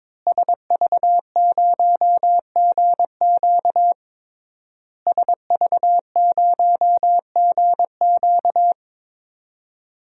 Morse code, heard as a keyed tone, about 22 wpm, 700 Hz. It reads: SV0GQ SV0GQ
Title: DXCC-100 Call Signs - 5 Characters - Repeated 3x - 22wpm